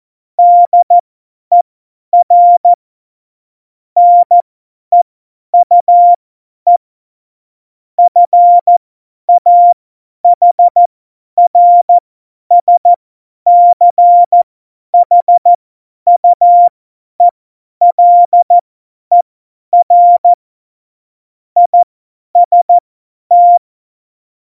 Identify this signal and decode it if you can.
Morse code, heard as a keyed tone, about 14 wpm, spaced as for 10 wpm, 700 Hz. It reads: DER NEUE FAHRSCHUELER IST
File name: Morsekurs / Witze / Witz 08. Januar 2019, 50 F70